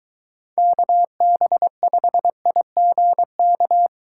Morse code, heard as a keyed tone, about 22 wpm, 700 Hz. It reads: KB5IGK